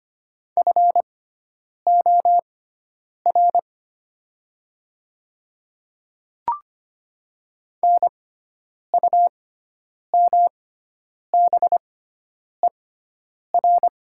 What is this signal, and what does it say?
Morse code, heard as a keyed tone, about 25 wpm, spaced as for 9 wpm, 700 Hz. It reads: FOR NUMBER
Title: 6x Character Spacing - Top 100 Words - 25wpm